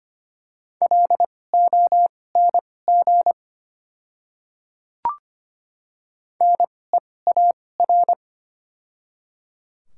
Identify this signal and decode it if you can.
Morse code, heard as a keyed tone, about 25 wpm, spaced as for 18 wpm, 700 Hz. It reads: LONG NEAR
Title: 2x Character Spacing - Top 200 Words - 25wpm